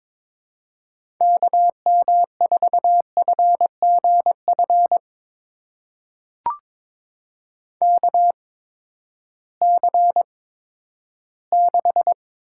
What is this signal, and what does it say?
Morse code, heard as a keyed tone, about 22 wpm, spaced as for 12 wpm, 700 Hz. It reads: KM4FGF KC6
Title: USA Call Signs - 6 Characters - 8x Character Spacing - 22wpm